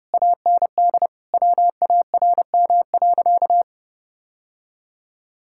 Morse code, heard as a keyed tone, about 30 wpm, 700 Hz. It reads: AND WARM.
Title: Sentences from Top 1600 Words 30wpm